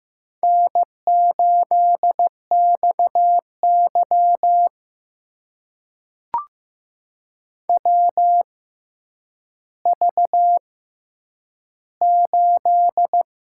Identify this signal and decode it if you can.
Morse code, heard as a keyed tone, about 15 wpm, spaced as for 9 wpm, 700 Hz. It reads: N8XY WV8
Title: USA Call Signs - 4 Characters - 6x Character Spacing - 15wpm